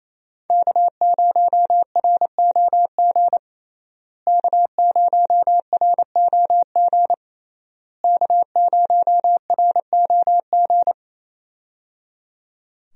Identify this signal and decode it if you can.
Morse code, heard as a keyed tone, about 28 wpm, 700 Hz. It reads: K0ROG K0ROG K0ROG
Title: USA Call Signs - 5 Characters - Repeated 3x - 28wpm